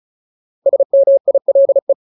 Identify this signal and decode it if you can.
Morse code, heard as a keyed tone, about 34 wpm, 550 Hz. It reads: SMILE
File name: Ultimate Word List / 35WPM Ultimate